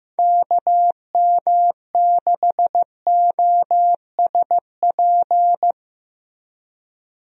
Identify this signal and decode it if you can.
Morse code, heard as a keyed tone, about 15 wpm, 700 Hz. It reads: KM6OSP
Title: USA Call Signs - 6 Characters - Repeated 2x - 15wpm